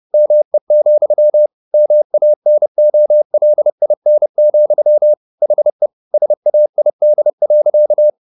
Morse code, heard as a keyed tone, about 30 wpm, 600 Hz. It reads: ME, MANOLIN, HE SAID.